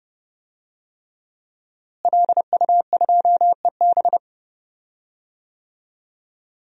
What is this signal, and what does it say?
Morse code, heard as a keyed tone, about 30 wpm, 700 Hz. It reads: LU2EB
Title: DXCC-100 Call Signs - 5 Characters - Repeated 3x - 30wpm